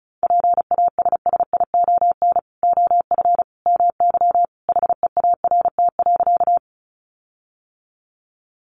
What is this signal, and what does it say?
Morse code, heard as a keyed tone, about 34 wpm, 700 Hz. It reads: PASSION OF MY HEART.
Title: Sentences from Top 1300 Words 35wpm